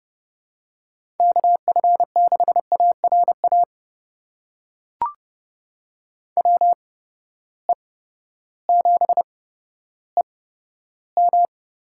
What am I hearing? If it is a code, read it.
Morse code, heard as a keyed tone, about 30 wpm, spaced as for 14 wpm, 700 Hz. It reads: KF6ARA WE7EM
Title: USA Call Signs - 8x Character Spacing - 30wpm